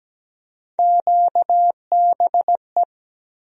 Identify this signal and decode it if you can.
Morse code, heard as a keyed tone, about 17 wpm, 700 Hz. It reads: QBE
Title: USA Call Sign Postfixes - 3 Letters 17wpm